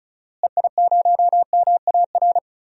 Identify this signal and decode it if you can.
Morse code, heard as a keyed tone, about 36 wpm, 700 Hz. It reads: EI0MAR
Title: DXCC-100 Call Signs - 6 Characters - Repeated 3x - 35wpm